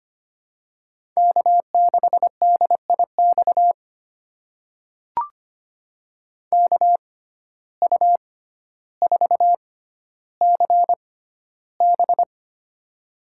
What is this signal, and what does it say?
Morse code, heard as a keyed tone, about 25 wpm, spaced as for 13 wpm, 700 Hz. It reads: K6DIX KU4CB
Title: USA Call Signs - 5 Characters - 6x Character Spacing - 25wpm